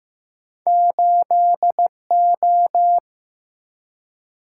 Morse code, heard as a keyed tone, about 15 wpm, 700 Hz. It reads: 8O
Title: Single-Letter Single-Number 15wpm